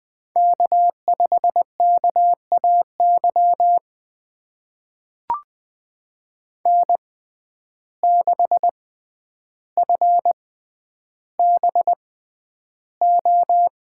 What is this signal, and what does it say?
Morse code, heard as a keyed tone, about 20 wpm, spaced as for 10 wpm, 700 Hz. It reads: K5KAY N6FBO
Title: USA Call Signs - 5 Characters - 6x Character Spacing - 20wpm